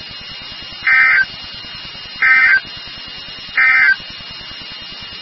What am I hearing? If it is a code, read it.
Emergency Alert System (SAME) tones, heard as harsh buzzing bursts. It is heard from 0.7 to 4.0 s.
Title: NOAA Weather Radio WXK36